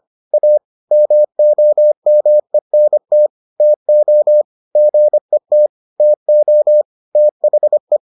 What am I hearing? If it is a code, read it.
Morse code, heard as a keyed tone, about 25 wpm, 600 Hz. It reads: A MOMENT TO GET TO THE